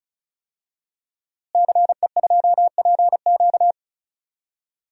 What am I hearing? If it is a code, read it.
Morse code, heard as a keyed tone, about 36 wpm, 700 Hz. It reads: CE2PQ
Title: DXCC-100 Call Signs - 5 Characters - Repeated 2x - 35wpm